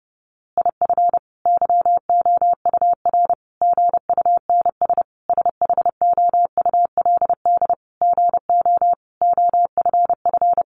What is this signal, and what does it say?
Morse code, heard as a keyed tone, about 30 wpm, 700 Hz. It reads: IF YOUR GUNS SHOULD GO OFF
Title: A Sound Of Thunder 30wpm